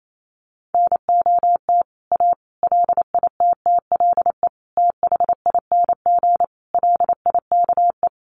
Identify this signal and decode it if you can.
Morse code, heard as a keyed tone, about 28 wpm, 700 Hz. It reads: NOT A LITTLE THING LIKE
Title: A Sound Of Thunder 28wpm